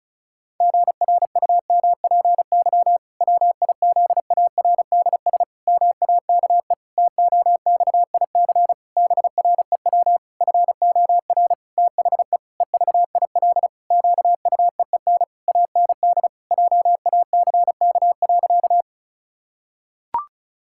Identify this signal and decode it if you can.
Morse code, heard as a keyed tone, about 36 wpm, 700 Hz. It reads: GRUMPY WIZARDS MAKE TOXIC BREW FOR THE EVIL QUEEN AND JACK.
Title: Pangram Challenge v2 - 35wpm